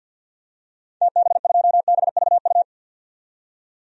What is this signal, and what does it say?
Morse code, heard as a keyed tone, about 53 wpm, 700 Hz. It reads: TB2BVU
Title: DXCC-100 Call Signs - 6 Characters - 50wpm